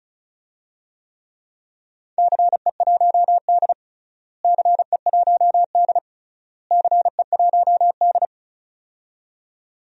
Morse code, heard as a keyed tone, about 36 wpm, 700 Hz. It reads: CE1D CE1D CE1D
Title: DXCC-100 Call Signs - 4 Characters - Repeated 3x - 35wpm